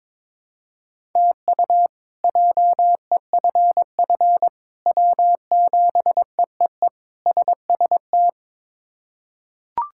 Morse code, heard as a keyed tone, about 22 wpm, 700 Hz. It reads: TU JEFF W7EEE SST
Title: K1USN SST QSO Practice - 22wpm